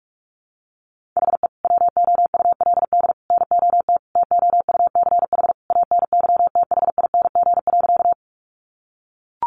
Morse code, heard as a keyed tone, about 44 wpm, 700 Hz. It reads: HE WOULD NOT TOUCH ANYTHING.
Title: Sentences from Top 1200 Words 45wpm